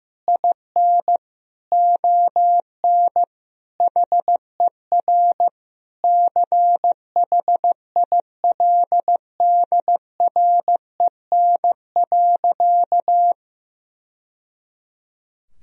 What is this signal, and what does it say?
Morse code, heard as a keyed tone, about 16 wpm, 700 Hz. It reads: IN ON HER CHILDREN.
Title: Sentences from Top 1000 Words - Review Part 3 - 15wpm